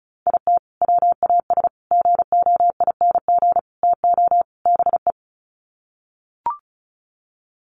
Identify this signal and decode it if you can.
Morse code, heard as a keyed tone, about 34 wpm, 700 Hz. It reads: IT WAS GOING TO BE
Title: Sets of 5 Words 35wpm